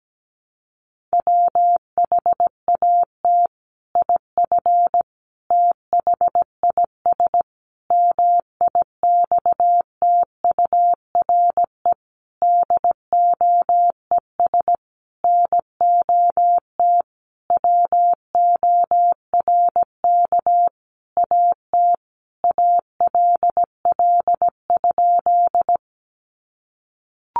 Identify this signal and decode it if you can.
Morse code, heard as a keyed tone, about 17 wpm, 700 Hz. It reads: WHAT IF THIS MIXTURE DOES NOT WORK AT ALL?
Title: Sentences from Top 1500 Words 17wpm